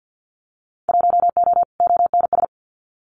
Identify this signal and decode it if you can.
Morse code, heard as a keyed tone, about 50 wpm, 700 Hz. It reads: 1O ONS
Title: Winter Field Day Exchanges Encore 50wpm